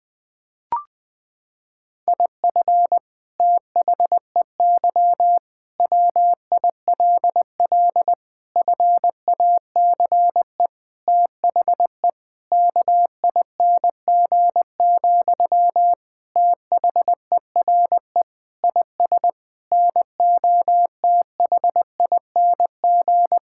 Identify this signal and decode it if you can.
Morse code, heard as a keyed tone, about 21 wpm, 700 Hz. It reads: IF THEY WILL FACE THE KING, THERE IS NOTHING
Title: Sentences from Top 1000 Words - Review Part 6 - 20wpm